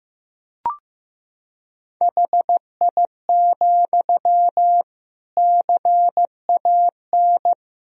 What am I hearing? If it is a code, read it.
Morse code, heard as a keyed tone, about 15 wpm, 700 Hz. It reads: HI, CAN
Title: Sentences from Top 1000 Words - Review Part 3 - 15wpm